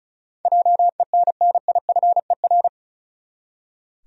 Morse code, heard as a keyed tone, about 36 wpm, 700 Hz. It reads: JENNIFER
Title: USA Common Names - 35wpm